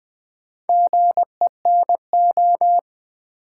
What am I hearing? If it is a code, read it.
Morse code, heard as a keyed tone, about 20 wpm, 700 Hz. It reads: GENO
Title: Build A Word - 2 Letters At A Time - 8-Letter Words - 20wpm